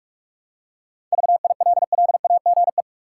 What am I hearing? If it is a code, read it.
Morse code, heard as a keyed tone, about 46 wpm, 700 Hz. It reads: VILLAGE
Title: Seven Letter Words - 45wpm